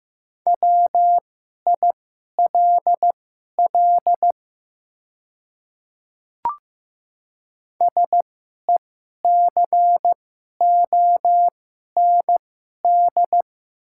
Morse code, heard as a keyed tone, about 15 wpm, spaced as for 11 wpm, 700 Hz. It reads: WILL SECOND
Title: 2x Character Spacing - Top 500 Words - 15wpm